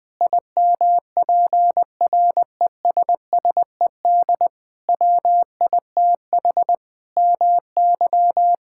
Morse code, heard as a keyed tone, about 20 wpm, 700 Hz. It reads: IMPRESSED WITH MY
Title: Sentences from Top 1600 Words 20wpm